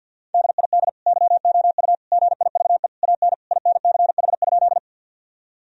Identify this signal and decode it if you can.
Morse code, heard as a keyed tone, about 50 wpm, 700 Hz. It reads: DID YOU GIVE AN INCH?
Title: Sentences from Top 1000 Words - Review Part 3 - 50wpm